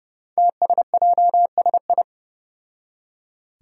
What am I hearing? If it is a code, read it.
Morse code, heard as a keyed tone, about 30 wpm, 700 Hz. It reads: TSJSI
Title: Five Challenging Letter-Number Combinations 30wpm